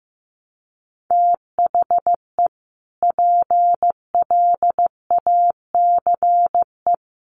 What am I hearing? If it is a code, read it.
Morse code, heard as a keyed tone, about 15 wpm, 700 Hz. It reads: THE PLACE
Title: Sentences from Top 1400 Words 15wpm